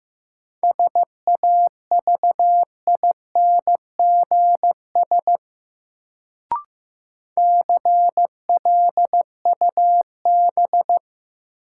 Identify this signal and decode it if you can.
Morse code, heard as a keyed tone, about 15 wpm, 700 Hz. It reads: SAVINGS CLUB
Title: Sentences from Top 1200 Words - New Word Drill Alt - 15wpm